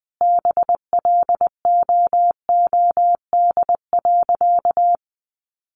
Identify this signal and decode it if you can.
Morse code, heard as a keyed tone, about 20 wpm, 700 Hz. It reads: BLOOD.
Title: Sentences from Top 1200 Words 20wpm